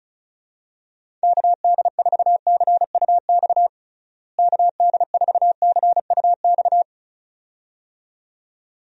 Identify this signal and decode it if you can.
Morse code, heard as a keyed tone, about 34 wpm, 700 Hz. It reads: KD4CUX KD4CUX
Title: USA Call Signs - 6 Characters - Repeated 2x - 35wpm